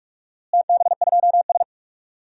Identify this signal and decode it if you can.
Morse code, heard as a keyed tone, about 46 wpm, 700 Hz. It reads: TB2S
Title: DXCC-100 Call Signs - 4 Characters - Repeated 3x - 45wpm